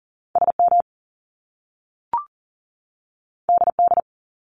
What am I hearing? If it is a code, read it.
Morse code, heard as a keyed tone, about 40 wpm, 700 Hz. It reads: SM DD
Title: Top Two Letter Combinations in Words 40wpm